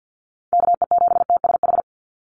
Morse code, heard as a keyed tone, about 50 wpm, 700 Hz. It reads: KE7TSH
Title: USA Call Signs 50wpm